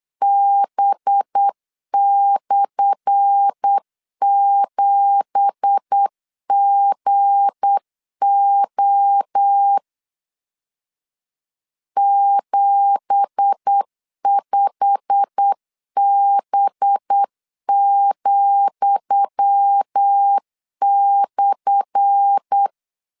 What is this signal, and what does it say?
Morse code, heard as a keyed tone, about 8 wpm, 800 Hz. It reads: B/7GO 75B,/
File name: LEZIONI / 10 SIGNS / 10D COMA 8wpm 3749